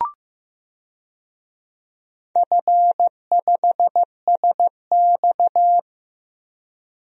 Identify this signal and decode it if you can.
Morse code, heard as a keyed tone, about 15 wpm, 700 Hz. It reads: F5SX
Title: DXCC-100 Call Signs - 4 Characters - Repeated 3x - 15wpm